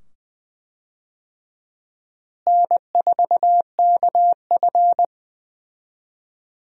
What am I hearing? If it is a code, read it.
Morse code, heard as a keyed tone, about 20 wpm, 700 Hz. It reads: N4KF
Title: USA Call Signs - 4 Characters - Repeated 2x - 20wpm